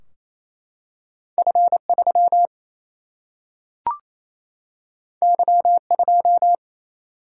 Morse code, heard as a keyed tone, about 28 wpm, 700 Hz. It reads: F 3 Y 2
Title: Single-Letter Single-Number 28wpm